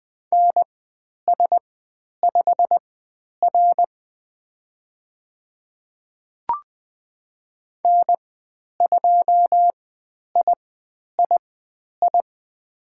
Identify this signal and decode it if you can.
Morse code, heard as a keyed tone, about 20 wpm, spaced as for 10 wpm, 700 Hz. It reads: NS5R N2III
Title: 102 Lesson 17g - Letter u - Call Signs 20wpm